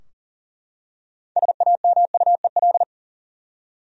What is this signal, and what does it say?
Morse code, heard as a keyed tone, about 40 wpm, 700 Hz. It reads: SAMUEL
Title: USA Common Names - 40wpm